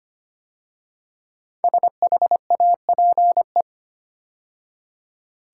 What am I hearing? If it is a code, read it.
Morse code, heard as a keyed tone, about 25 wpm, 700 Hz. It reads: SHAPE
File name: 5 Letter Words - First 3 Letters Given - 25wpm